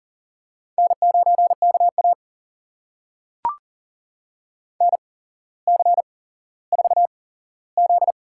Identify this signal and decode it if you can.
Morse code, heard as a keyed tone, about 40 wpm, spaced as for 17 wpm, 700 Hz. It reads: N9KA NC4Z
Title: USA Call Signs - 4 Characters - 8x Character Spacing - 40wpm